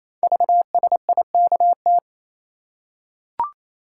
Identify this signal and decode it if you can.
Morse code, heard as a keyed tone, about 28 wpm, 700 Hz. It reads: VSIKT
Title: Code Groups - 5 Letters - 28wpm